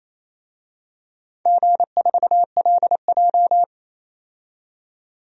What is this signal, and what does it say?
Morse code, heard as a keyed tone, about 28 wpm, 700 Hz. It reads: G4LJ
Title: DXCC-100 Call Signs - 4 Characters - 28wpm